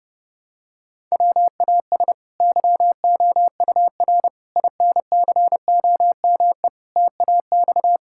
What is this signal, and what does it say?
Morse code, heard as a keyed tone, about 32 wpm, 700 Hz. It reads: WAS YOUR INCOME TAX
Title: Sentences from Top 1000 Words - Review Part 5 - 30wpm